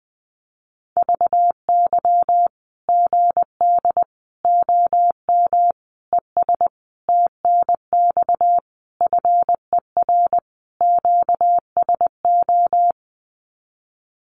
Morse code, heard as a keyed tone, about 19 wpm, 700 Hz. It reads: VY GD OM ES TNX FER QSO
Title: Common QSO Phrases 20wpm